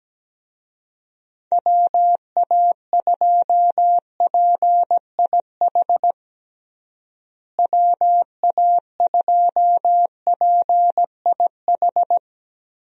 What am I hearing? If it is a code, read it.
Morse code, heard as a keyed tone, about 17 wpm, 700 Hz. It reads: WA2PIH WA2PIH
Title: USA Call Signs - 6 Characters - Repeated 3x - 17wpm